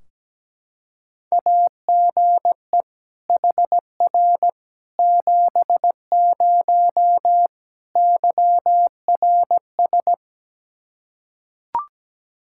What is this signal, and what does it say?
Morse code, heard as a keyed tone, about 17 wpm, 700 Hz. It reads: AGE HR 70 YRS
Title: Practice QSO - Exchange 3 - More Info - 17wpm